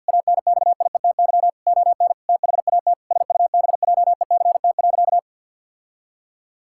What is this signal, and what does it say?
Morse code, heard as a keyed tone, about 45 wpm, 700 Hz. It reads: ANXIETY ON THAT SUBJECT.
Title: Sentences from Top 1600 Words 50wpm